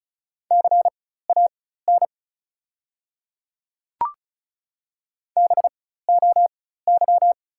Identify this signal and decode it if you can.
Morse code, heard as a keyed tone, about 36 wpm, spaced as for 16 wpm, 700 Hz. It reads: CAN BOY
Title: 4x Character Spacing - Top 200 Words - 35wpm